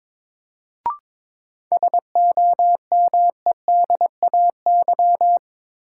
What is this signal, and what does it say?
Morse code, heard as a keyed tone, about 22 wpm, 700 Hz. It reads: SOMEDAY
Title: Sentences from Top 1600 Words - New Word Drill - 22wpm